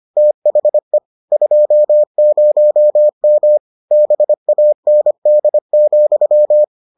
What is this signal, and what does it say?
Morse code, heard as a keyed tone, about 25 wpm, 600 Hz. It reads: THE 20M BAND,